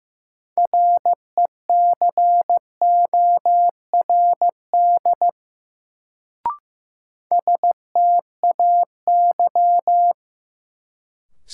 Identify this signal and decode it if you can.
Morse code, heard as a keyed tone, about 15 wpm, 700 Hz. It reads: RECORD STAY
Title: Sentences from Top 500 Words - New Word Drill - 15wpm